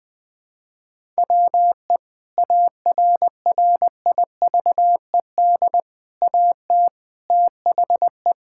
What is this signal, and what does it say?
Morse code, heard as a keyed tone, about 20 wpm, 700 Hz. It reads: WE ARRIVED AT THE
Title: Sentences with 4-Word Phrases - 20wpm